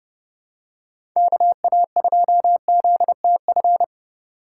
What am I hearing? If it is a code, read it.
Morse code, heard as a keyed tone, about 30 wpm, 700 Hz. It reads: KA2ZTF
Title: USA Call Signs - 6 Characters - 3x Character Spacing - 30wpm